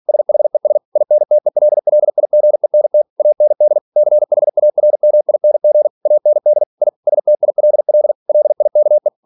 Morse code, heard as a keyed tone, about 46 wpm, 600 Hz. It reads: SHES INTELLIGENT AND CHARMING AND I STILL LIKE
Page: Never 47wpm